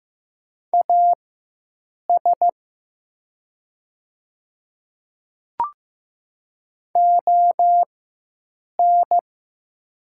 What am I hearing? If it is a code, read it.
Morse code, heard as a keyed tone, about 15 wpm, spaced as for 7 wpm, 700 Hz. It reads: AS ON
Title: 4x Character Spacing - Top 100 Words - 15wpm